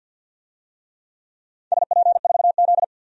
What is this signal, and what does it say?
Morse code, heard as a keyed tone, about 48 wpm, 700 Hz. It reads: SP4Z
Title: DXCC-100 Call Signs - 4 Characters - 50wpm